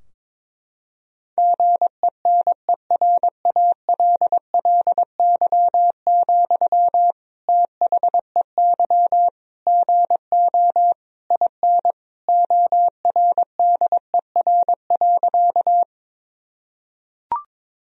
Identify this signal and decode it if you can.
Morse code, heard as a keyed tone, about 22 wpm, 700 Hz. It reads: GENERALLY, THEY GO IN ORDER.
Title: Sentences from Top 1000 Words - Review Part 4 - 22wpm